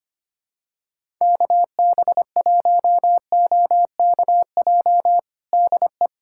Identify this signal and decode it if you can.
Morse code, heard as a keyed tone, about 25 wpm, 700 Hz. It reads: KB1OKJ DE
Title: USA Callsign DE Callsign 25wpm